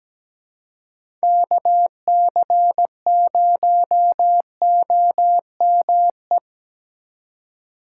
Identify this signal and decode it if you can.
Morse code, heard as a keyed tone, about 17 wpm, 700 Hz. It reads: KC0OME
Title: USA Call Signs - 6 Characters - 2x Character Spacing - 17wpm